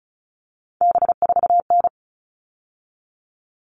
Morse code, heard as a keyed tone, about 34 wpm, 700 Hz. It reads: B4N
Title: All Cut Numbers - 3-Digit 35wpm